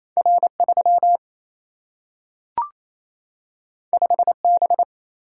Morse code, heard as a keyed tone, about 28 wpm, 700 Hz. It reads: R 3 5 B
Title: Single-Letter Single-Number 28wpm